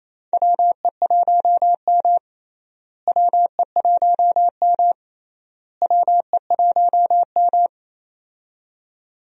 Morse code, heard as a keyed tone, about 28 wpm, 700 Hz. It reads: WE1M WE1M WE1M
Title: USA Call Signs - 4 Characters - Repeated 3x - 28wpm